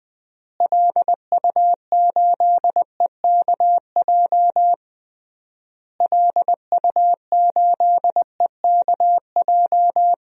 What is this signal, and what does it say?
Morse code, heard as a keyed tone, about 20 wpm, 700 Hz. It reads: LU8EKJ LU8EKJ
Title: DXCC-100 Call Signs - 6 Characters - Repeated 3x - 20wpm